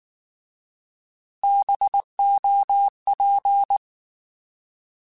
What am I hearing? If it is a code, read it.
Morse code, heard as a keyed tone, about 19 wpm, 800 Hz. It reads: BOP
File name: Alphabetical / Alphabetical mp3s / Bs @18